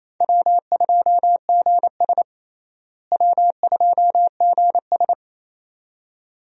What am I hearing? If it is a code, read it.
Morse code, heard as a keyed tone, about 28 wpm, 700 Hz. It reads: W2GS W2GS
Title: USA Call Signs - 4 Characters - Repeated 2x - 28wpm